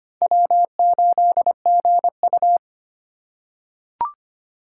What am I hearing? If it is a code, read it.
Morse code, heard as a keyed tone, about 25 wpm, 700 Hz. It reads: W8GU
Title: K1USN SST QSO Practice - 25wpm